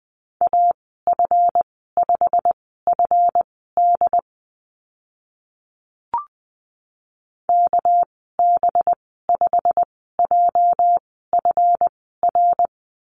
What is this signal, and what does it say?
Morse code, heard as a keyed tone, about 20 wpm, spaced as for 15 wpm, 700 Hz. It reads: AF5FD KB5JFR
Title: 238 Lesson 38g - Letter q - Call Signs 20wpm